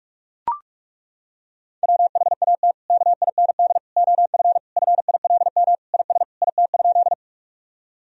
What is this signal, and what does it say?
Morse code, heard as a keyed tone, about 41 wpm, 700 Hz. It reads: WHAT KIND OF FILM IS IT?
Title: Sentences from Top 1000 Words - Review Part 3 - 45wpm